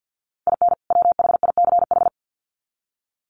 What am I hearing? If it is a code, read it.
Morse code, heard as a keyed tone, about 50 wpm, 700 Hz. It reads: IN WHICH
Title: Sets of 2 Words 50wpm